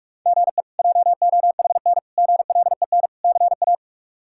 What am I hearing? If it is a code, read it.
Morse code, heard as a keyed tone, about 46 wpm, 700 Hz. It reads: GE JOHN GLEN CA
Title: K1USN SST QSO Practice - 45wpm